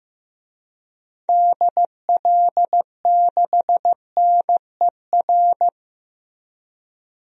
Morse code, heard as a keyed tone, about 15 wpm, 700 Hz. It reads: DL6NER
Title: DXCC-100 Call Signs - 6 Characters - Repeated 3x - 15wpm